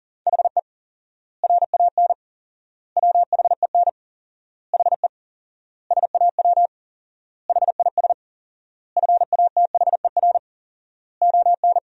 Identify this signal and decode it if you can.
Morse code, heard as a keyed tone, about 40 wpm, 700 Hz. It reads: HE RAN WHEN HE SAW HIS FATHER ON
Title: 4x Word-Spacing - Sentences from Top 500 Words - 40wpm